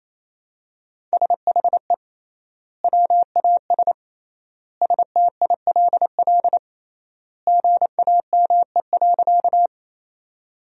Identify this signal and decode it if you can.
Morse code, heard as a keyed tone, about 27 wpm, 700 Hz. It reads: SHE WAS STILL GAME.
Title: 3x Word-Spacing - Sentences from Top 500 Words - 28wpm